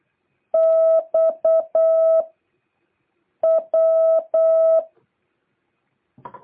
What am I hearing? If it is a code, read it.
Morse code, heard as a keyed tone, about 8 wpm, 660 Hz. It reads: X W